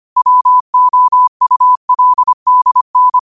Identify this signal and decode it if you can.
Morse code, heard as a keyed tone, about 25 wpm, 1000 Hz. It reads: WOULDN